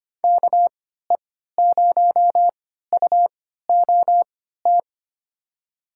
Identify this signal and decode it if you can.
Morse code, heard as a keyed tone, about 25 wpm, spaced as for 14 wpm, 700 Hz. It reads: KE0UOT
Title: USA Call Signs - 6 Characters - 3x Character Spacing - 25wpm